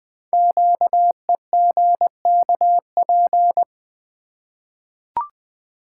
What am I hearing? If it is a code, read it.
Morse code, heard as a keyed tone, about 20 wpm, 700 Hz. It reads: QEGKP
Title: Code Groups - 5 Letters - 20wpm